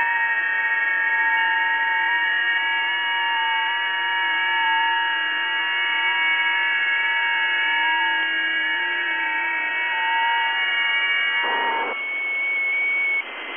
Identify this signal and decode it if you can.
FT8.
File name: tests / websdr test13